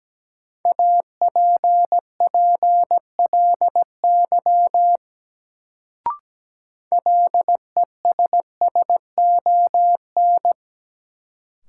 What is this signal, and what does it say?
Morse code, heard as a keyed tone, about 17 wpm, 700 Hz. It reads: APPLY LESSON
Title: Sentences from Top 1200 Words - New Word Drill Alt - 17wpm